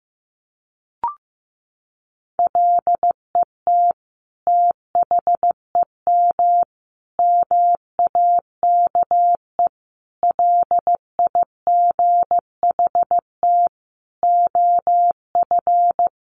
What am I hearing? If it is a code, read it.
Morse code, heard as a keyed tone, about 15 wpm, 700 Hz. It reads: LET THEM MAKE LIGHT OF
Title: Sentences from Top 1300 Words 15wpm